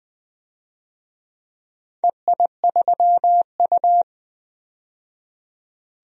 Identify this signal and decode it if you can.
Morse code, heard as a keyed tone, about 20 wpm, 700 Hz. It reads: EI3U
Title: DXCC-100 Call Signs - 4 Characters - 20wpm